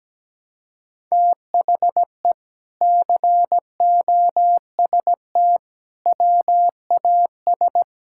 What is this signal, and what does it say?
Morse code, heard as a keyed tone, about 17 wpm, 700 Hz. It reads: THE COST WAS
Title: Sentences from Top 1000 Words - Review Part 6 - 17wpm